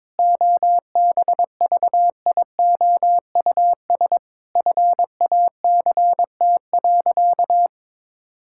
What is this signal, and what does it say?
Morse code, heard as a keyed tone, about 22 wpm, 700 Hz. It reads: OBVIOUS FACT.
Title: Sentences from Top 1600 Words 22wpm